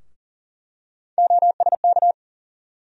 Morse code, heard as a keyed tone, about 40 wpm, 700 Hz. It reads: OSK
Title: USA Call Sign Postfixes - 3 Letters 40wpm